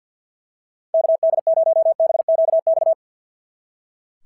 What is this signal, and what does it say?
Morse code, heard as a keyed tone, about 46 wpm, 650 Hz. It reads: KD0BQX
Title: SOTA QSO Practice 50wpm